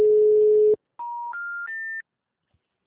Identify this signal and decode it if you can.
Special information tone.